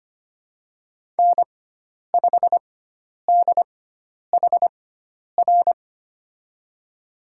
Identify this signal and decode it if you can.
Morse code, heard as a keyed tone, about 25 wpm, spaced as for 10 wpm, 700 Hz. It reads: N5DHR
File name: Lesson 16g - Letter c - Call Signs 25wpm